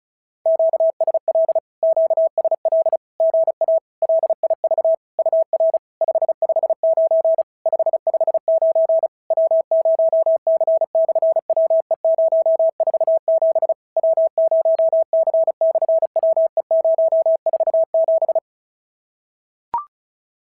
Morse code, heard as a keyed tone, about 33 wpm, 650 Hz. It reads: QSL QSL GA LIV UR 559 559 W0C/WE047 W0C/WE047
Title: SOTA QSO Practice 35wpm